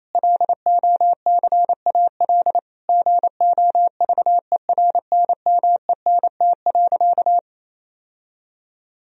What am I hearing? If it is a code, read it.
Morse code, heard as a keyed tone, about 27 wpm, 700 Hz. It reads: LOCAL GOVERNMENT.
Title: Sentences from Top 1000 Words - Review Part 3 - 28wpm